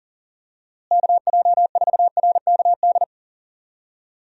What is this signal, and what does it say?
Morse code, heard as a keyed tone, about 40 wpm, 700 Hz. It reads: KJ4RKD
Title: USA Call Signs - 6 Characters - Repeated 3x - 40wpm